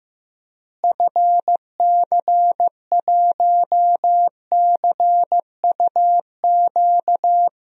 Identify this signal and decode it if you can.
Morse code, heard as a keyed tone, about 15 wpm, 700 Hz. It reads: FC1CUQ
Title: DXCC-100 Call Signs - 6 Characters - Repeated 3x - 15wpm